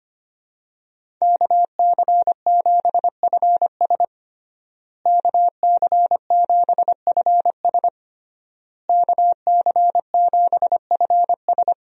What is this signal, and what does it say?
Morse code, heard as a keyed tone, about 25 wpm, 700 Hz. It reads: KC7FS KC7FS KC7FS
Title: USA Call Signs - 5 Characters - Repeated 3x - 25wpm